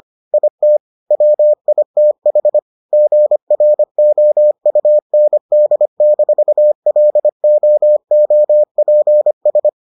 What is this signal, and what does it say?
Morse code, heard as a keyed tone, about 25 wpm, 600 Hz. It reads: IT WITH GROUND-LOOPS